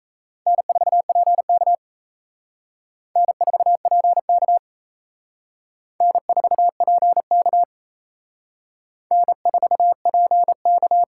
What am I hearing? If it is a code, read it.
Morse code, heard as a keyed tone, about 35 wpm, 700 Hz. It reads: N4PK N4PK N4PK N4PK
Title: Speed Racing - USA 1x2 Call Signs 28wpm